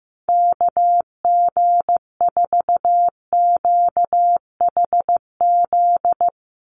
Morse code, heard as a keyed tone, about 15 wpm, 700 Hz. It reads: KG4QHZ
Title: USA Call Signs 15wpm